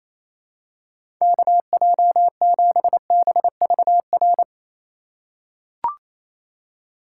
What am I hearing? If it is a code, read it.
Morse code, heard as a keyed tone, about 27 wpm, 700 Hz. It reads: KJ7BVR
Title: USA Call Signs - 6 Characters - 4x Character Spacing - 28wpm